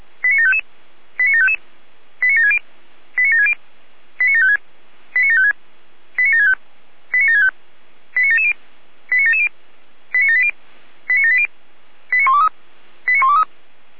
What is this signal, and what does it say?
Five-tone selective call (ZVEI), heard as quick runs of five tones, tones of about 80 ms. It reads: 89755 89755 89760 89760 89756 89756 89765 89765 89770 89770 89780 89780 89123 89123
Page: /455/SDR/ZVEI/89730/